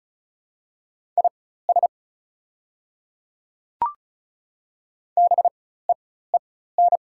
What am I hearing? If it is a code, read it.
Morse code, heard as a keyed tone, about 36 wpm, spaced as for 16 wpm, 700 Hz. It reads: IS BEEN
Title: 4x Character Spacing - Top 100 Words - 35wpm